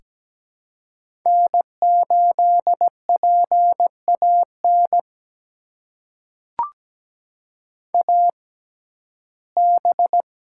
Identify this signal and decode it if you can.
Morse code, heard as a keyed tone, about 17 wpm, spaced as for 12 wpm, 700 Hz. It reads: N8PAN AB
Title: USA Call Signs - 5 Characters - 6x Character Spacing - 17wpm